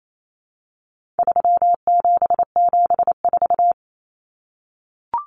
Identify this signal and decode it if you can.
Morse code, heard as a keyed tone, about 28 wpm, 700 Hz. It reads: 3774
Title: Common Cut Numbers - 4-Digit 28wpm